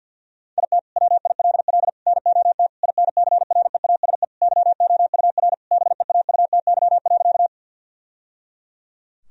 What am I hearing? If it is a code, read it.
Morse code, heard as a keyed tone, about 50 wpm, 700 Hz. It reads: IT WILL NOT INCREASE YOUR BEAUTY.
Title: Sentences from Top 1000 Words - Review Part 4 - 50wpm